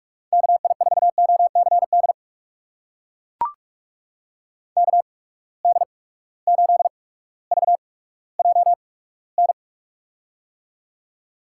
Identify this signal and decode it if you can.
Morse code, heard as a keyed tone, about 42 wpm, spaced as for 19 wpm, 700 Hz. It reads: KI4OCD KD8VJN
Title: USA Call Signs - 8x Character Spacing - 45wpm